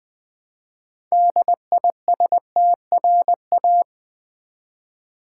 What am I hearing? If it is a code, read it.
Morse code, heard as a keyed tone, about 20 wpm, 700 Hz. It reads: DISTRA
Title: Build A Word - 3 Letters At A Time - 8-Letter Words - 20wpm